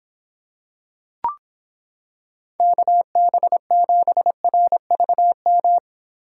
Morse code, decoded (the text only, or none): KB7RVM